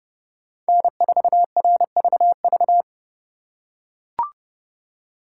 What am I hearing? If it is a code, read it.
Morse code, heard as a keyed tone, about 28 wpm, 700 Hz. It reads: N4RVV